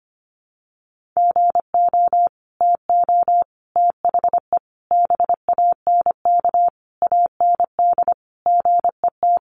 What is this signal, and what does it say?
Morse code, heard as a keyed tone, about 25 wpm, 700 Hz. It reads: GO TO THE BANK AND GET
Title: Sentences from Top 1500 Words 25wpm